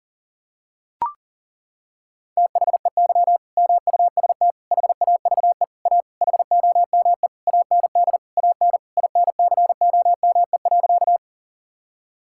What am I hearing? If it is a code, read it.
Morse code, heard as a keyed tone, about 40 wpm, 700 Hz. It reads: THEY MUST HAVE A HOME AND AN INCOME.
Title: Sentences from Top 1000 Words - Review Part 5 - 40wpm